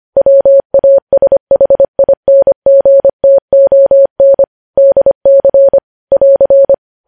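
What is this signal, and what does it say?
Morse code, heard as a keyed tone, about 25 wpm, 550 Hz. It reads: WASHINGTON DC <AR>
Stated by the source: Morse code news headlines sent at 25 wpm.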